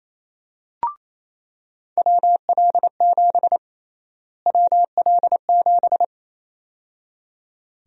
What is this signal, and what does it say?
Morse code, heard as a keyed tone, about 28 wpm, 700 Hz. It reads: WL7 WL7
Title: USA Call Sign Prefixes - Repeated 2x - 28wpm